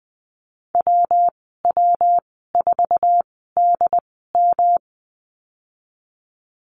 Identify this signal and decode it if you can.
Morse code, heard as a keyed tone, about 20 wpm, spaced as for 15 wpm, 700 Hz. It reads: WW4DM
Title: 186 Lesson 30g - Letter b - Call Signs 20wpm